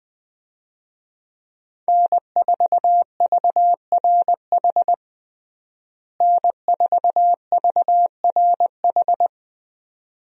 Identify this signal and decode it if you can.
Morse code, heard as a keyed tone, about 20 wpm, 700 Hz. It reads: N4VRH N4VRH
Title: USA Call Signs - 5 Characters - Repeated 2x - 20wpm